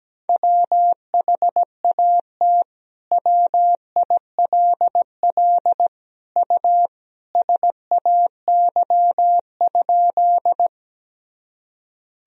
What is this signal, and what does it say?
Morse code, heard as a keyed tone, about 17 wpm, 700 Hz. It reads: WHAT WILL U SAY?
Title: Sentences from Top 200 Words and Abbreviations 17wpm